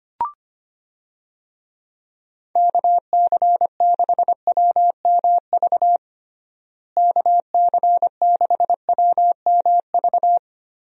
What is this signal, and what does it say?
Morse code, heard as a keyed tone, about 25 wpm, 700 Hz. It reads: KC6WMV KC6WMV
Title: USA Call Signs - 6 Characters - Repeated 3x - 25wpm